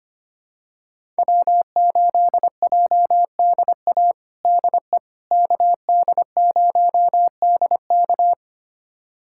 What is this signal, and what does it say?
Morse code, heard as a keyed tone, about 25 wpm, 700 Hz. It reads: W8JDA DE KD0DK
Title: USA Callsign DE Callsign - 5 Characters - 25wpm